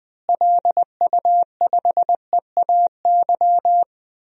Morse code, heard as a keyed tone, about 21 wpm, 700 Hz. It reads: LU5EAY
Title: DXCC-100 Call Signs - 6 Characters - Repeated 3x - 20wpm